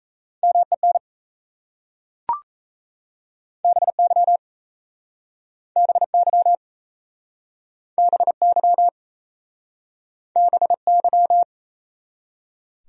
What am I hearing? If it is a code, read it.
Morse code, heard as a keyed tone, about 37 wpm, 700 Hz. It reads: MEN BY BY BY BY